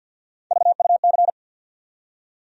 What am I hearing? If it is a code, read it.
Morse code, heard as a keyed tone, about 48 wpm, 700 Hz. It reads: VUC
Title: USA Call Sign Postfixes - 3 Letters 50wpm